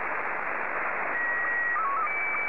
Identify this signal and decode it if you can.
SSTV (slow-scan television), Scottie 1.